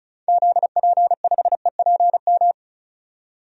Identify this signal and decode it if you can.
Morse code, heard as a keyed tone, about 36 wpm, 700 Hz. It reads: ZP5EPM
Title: DXCC-100 Call Signs - 6 Characters - 35wpm